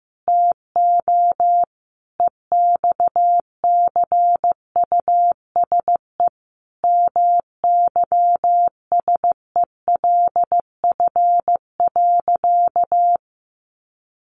Morse code, heard as a keyed tone, about 15 wpm, 700 Hz. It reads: TO EXCUSE MYSELF.
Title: Sentences from Top 1300 Words 15wpm